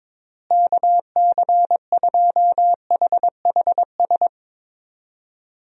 Morse code, heard as a keyed tone, about 22 wpm, 700 Hz. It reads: KC2HHS
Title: USA Call Signs - 6 Characters - Repeated 2x - 22wpm